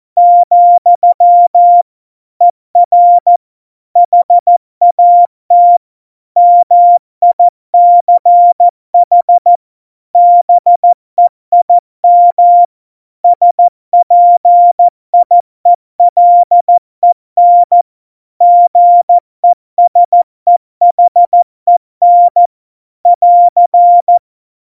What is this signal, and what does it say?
Morse code, heard as a keyed tone, about 14 wpm, 700 Hz. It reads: , ER HAT MICH BEIM SPIELEN GESEHEN <AR>
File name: Morsekurs / Witze / MP3, Witz 10 April 2018, 70 F80